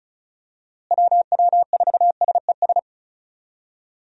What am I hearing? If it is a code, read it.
Morse code, heard as a keyed tone, about 36 wpm, 700 Hz. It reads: WW4SES
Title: USA Call Signs - 6 Characters - Repeated 2x - 35wpm